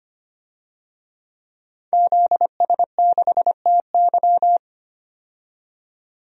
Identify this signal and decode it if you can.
Morse code, heard as a keyed tone, about 25 wpm, 700 Hz. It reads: ZS6TY
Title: DXCC-100 Call Signs - 5 Characters - 25wpm